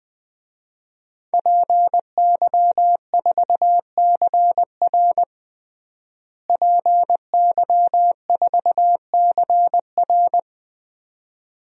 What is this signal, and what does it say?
Morse code, heard as a keyed tone, about 21 wpm, 700 Hz. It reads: PY4CR PY4CR
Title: DXCC-100 Call Signs - 5 Characters - Repeated 2x - 20wpm